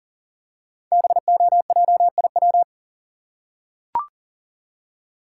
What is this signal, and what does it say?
Morse code, heard as a keyed tone, about 41 wpm, 700 Hz. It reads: BOJIW
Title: Code Groups - 5 Letters - 40wpm